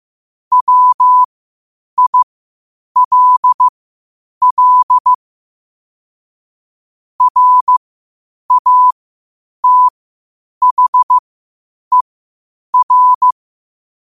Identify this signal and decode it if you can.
Morse code, heard as a keyed tone, about 15 wpm, spaced as for 8 wpm, 1000 Hz. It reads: WILL RATHER
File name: 8wpm